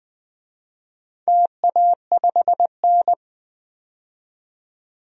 Morse code, heard as a keyed tone, about 21 wpm, 700 Hz. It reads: TA5N
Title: DXCC-100 Call Signs - 4 Characters - 20wpm